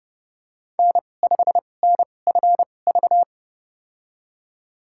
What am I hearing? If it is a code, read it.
Morse code, heard as a keyed tone, about 32 wpm, spaced as for 22 wpm, 700 Hz. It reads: N5NFV
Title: USA Call Signs - 5 Characters - 2x Character Spacing - 30wpm